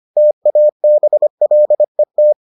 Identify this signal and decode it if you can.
Morse code, heard as a keyed tone, about 25 wpm, 600 Hz. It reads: TABLET